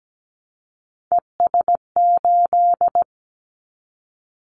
Morse code, heard as a keyed tone, about 17 wpm, 700 Hz. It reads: ES8
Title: Callsign Prefixes for Top 50 DXCC - 17wpm